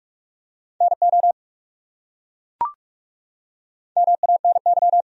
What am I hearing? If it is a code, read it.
Morse code, heard as a keyed tone, about 43 wpm, 700 Hz. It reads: NO MANY
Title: Top 100 Words 45wpm